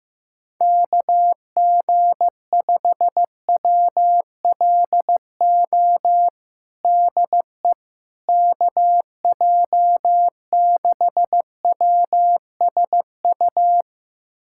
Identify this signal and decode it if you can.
Morse code, heard as a keyed tone, about 15 wpm, 700 Hz. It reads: KG5WLO DE KJ6WSU
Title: USA Callsign DE Callsign 15wpm